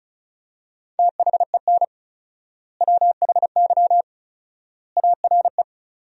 Morse code, heard as a keyed tone, about 36 wpm, 700 Hz. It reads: THEN WHY ARE
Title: 4x Word-Spacing - Sentences from Top 200 Words - 35wpm